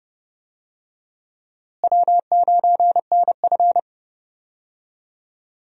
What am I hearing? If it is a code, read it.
Morse code, heard as a keyed tone, about 30 wpm, 700 Hz. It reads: W9NF
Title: USA Call Signs - 4 Characters - Repeated 2x - 30wpm